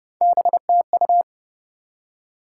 Morse code, heard as a keyed tone, about 28 wpm, 700 Hz. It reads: BTU
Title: Top 180 QSO Elements 30wpm